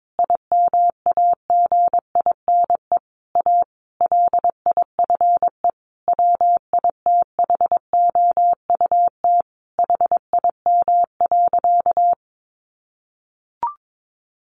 Morse code, heard as a keyed tone, about 22 wpm, 700 Hz. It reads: IMAGINE A LIFE WITHOUT HIM.
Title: Sentences from Top 1400 Words 22wpm